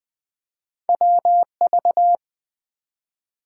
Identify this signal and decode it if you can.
Morse code, heard as a keyed tone, about 20 wpm, 700 Hz. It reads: WV
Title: US States and Canadian Provinces 20wpm